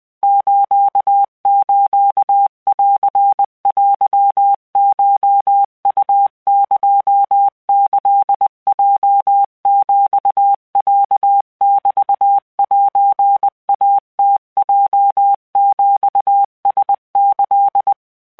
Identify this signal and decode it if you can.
Morse code (Japanese Wabun code), heard as a keyed tone, about 20 wpm, 800 Hz. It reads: ススンテコウエキヲヒロメセイムヲヒラキ
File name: keuikuchiyokugo20wptan